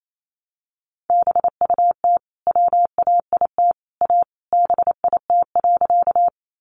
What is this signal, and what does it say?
Morse code, heard as a keyed tone, about 28 wpm, 700 Hz. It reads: BUT WAIT A BIT.
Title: Sentences from Top 900 Words 28wpm